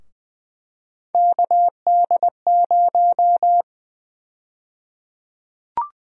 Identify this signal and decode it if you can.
Morse code, heard as a keyed tone, about 20 wpm, 700 Hz. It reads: KD0
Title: USA Call Sign Prefixes - Repeated 2x - 20wpm